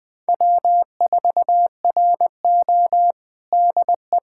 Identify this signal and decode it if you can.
Morse code, heard as a keyed tone, about 20 wpm, 700 Hz. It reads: W4RO DE
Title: USA Callsign DE Callsign - 4 Characters - 20wpm